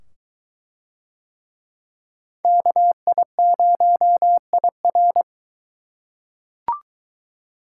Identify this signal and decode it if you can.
Morse code, heard as a keyed tone, about 20 wpm, 700 Hz. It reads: KI0IR